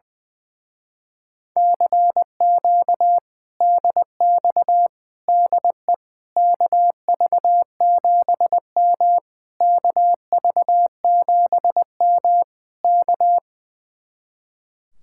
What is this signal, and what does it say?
Morse code, heard as a keyed tone, about 20 wpm, 700 Hz. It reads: CQ DX DE KV7M KV7M K
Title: Speed-Racing Common QSO Phrases Encore - 20wpm